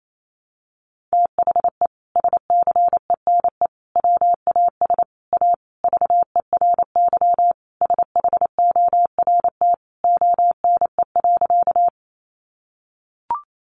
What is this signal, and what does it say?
Morse code, heard as a keyed tone, about 28 wpm, 700 Hz. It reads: THE SCENE WAS A VERY SHORT ONE.
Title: Sentences from Top 1100 Words 28wpm